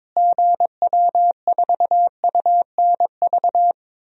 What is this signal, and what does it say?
Morse code, heard as a keyed tone, about 22 wpm, 700 Hz. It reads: GW4UNV
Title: DXCC-100 Call Signs - 6 Characters - Repeated 3x - 22wpm